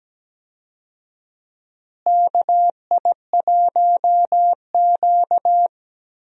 Morse code, heard as a keyed tone, about 17 wpm, 700 Hz. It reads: KI1Q
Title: USA Call Signs - 4 Characters - Repeated 2x - 17wpm